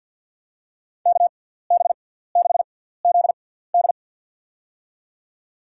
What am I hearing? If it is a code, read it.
Morse code, heard as a keyed tone, about 50 wpm, spaced as for 17 wpm, 700 Hz. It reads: KB6ZD
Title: USA Call Signs - 6x Character Spacing - 50wpm